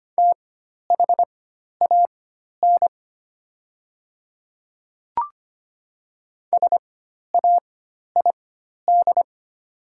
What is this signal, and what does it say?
Morse code, heard as a keyed tone, about 25 wpm, spaced as for 12 wpm, 690 Hz. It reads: THAN SAID
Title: 4x Character Spacing - Top 100 Words - 25wpm